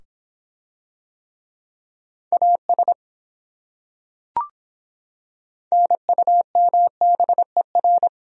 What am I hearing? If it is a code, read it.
Morse code, heard as a keyed tone, about 24 wpm, 700 Hz. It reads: AS NUMBER